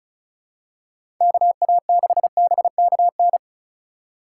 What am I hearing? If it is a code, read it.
Morse code, heard as a keyed tone, about 36 wpm, 690 Hz. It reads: KA6BKN
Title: USA Call Signs - 6 Characters - 2x Character Spacing - 35wpm